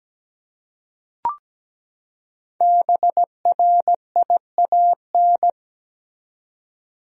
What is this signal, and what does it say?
Morse code, heard as a keyed tone, about 17 wpm, 700 Hz. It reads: BRIAN
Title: USA Common Names - 17wpm